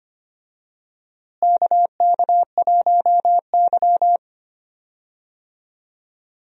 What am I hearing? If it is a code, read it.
Morse code, heard as a keyed tone, about 25 wpm, 700 Hz. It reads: KK1Y
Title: USA Call Signs - 4 Characters - Repeated 2x - 25wpm